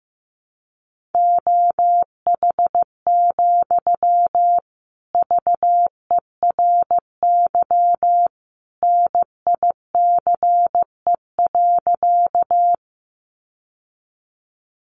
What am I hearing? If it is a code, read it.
Morse code, heard as a keyed tone, about 15 wpm, 700 Hz. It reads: OH, VERY NICE.
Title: Sentences from Top 1100 Words 15wpm